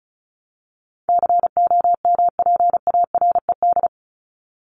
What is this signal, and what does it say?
Morse code, heard as a keyed tone, about 34 wpm, 700 Hz. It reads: COMPARED
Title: ED Suffix - Top 100 Words with Base Words 35wpm